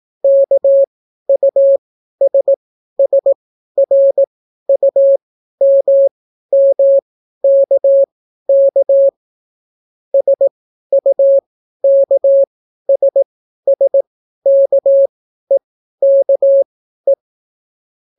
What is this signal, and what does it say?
Morse code, heard as a keyed tone, about 18 wpm, spaced as for 12 wpm, 550 Hz. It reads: KUSSRUMMKK SUKSSKEKE